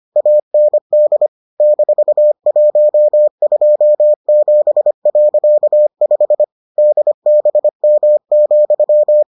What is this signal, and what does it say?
Morse code, heard as a keyed tone, about 25 wpm, 600 Hz. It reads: AND -127.5 DBM,